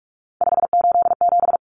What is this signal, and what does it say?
Morse code, heard as a keyed tone, about 44 wpm, 700 Hz. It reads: 587
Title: Numbers - 3-Digit 45wpm